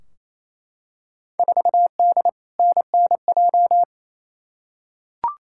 Morse code, heard as a keyed tone, about 28 wpm, 700 Hz. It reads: 4D NNJ
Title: Field Day Exchanges Encore - 28wpm